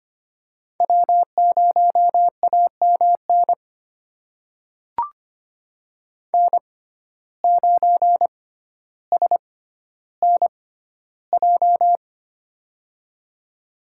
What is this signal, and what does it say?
Morse code, heard as a keyed tone, about 25 wpm, spaced as for 13 wpm, 700 Hz. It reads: W0AMN N9SNJ
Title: USA Call Signs - 5 Characters - 6x Character Spacing - 25wpm